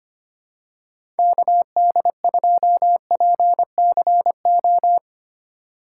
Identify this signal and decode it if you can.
Morse code, heard as a keyed tone, about 25 wpm, 700 Hz. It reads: KD2PCO
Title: USA Call Signs - 6 Characters - 2x Character Spacing - 25wpm